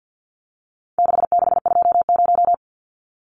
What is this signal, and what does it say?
Morse code, heard as a keyed tone, about 50 wpm, 700 Hz. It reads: 6620
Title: Numbers - 4-Digit 50wpm